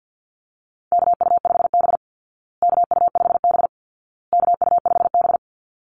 Morse code, heard as a keyed tone, about 50 wpm, 700 Hz. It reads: KU5B KU5B KU5B
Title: Call Signs Repeated 2x - 50wpm